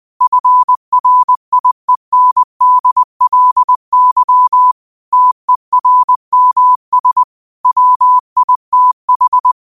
Morse code, heard as a keyed tone, about 20 wpm, 1000 Hz. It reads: FRIENDLY TERMS WITH
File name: qotdfiles / 20wpm